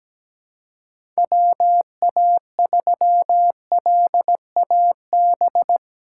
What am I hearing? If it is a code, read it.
Morse code, heard as a keyed tone, about 17 wpm, 700 Hz. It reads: WA3LAB
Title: K1USN SST QSO Practice - 17wpm